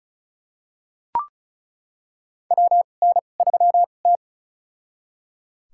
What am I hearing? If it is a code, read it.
Morse code, heard as a keyed tone, about 34 wpm, spaced as for 25 wpm, 700 Hz. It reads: WN3T
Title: USA Call Signs - 4 Characters - 2x Character Spacing - 35wpm